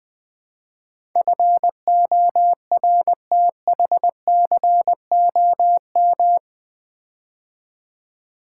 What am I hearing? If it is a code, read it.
Morse code, heard as a keyed tone, about 20 wpm, 700 Hz. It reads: FORTHCOM
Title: Build A Word - 4 Letters At A Time - 11-Letter Words - 20wpm